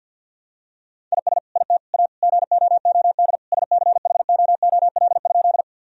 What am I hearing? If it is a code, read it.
Morse code, heard as a keyed tone, about 52 wpm, 700 Hz. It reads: IS IT A GOOD SCHOOL?
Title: Sentences from Top 1000 Words - Review Part 1 - 50wpm